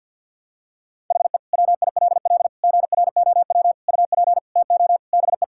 Morse code, heard as a keyed tone, about 50 wpm, 700 Hz. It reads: HE WILL GROW UP TO BE
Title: Sentences from Top 1000 Words - Review Part 6 - 50wpm